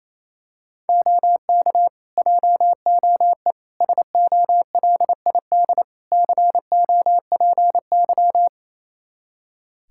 Morse code, heard as a keyed tone, about 28 wpm, 700 Hz. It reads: OK JOE SOLID COPY
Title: Practice QSO - Exchange 3 - More Info - 28wpm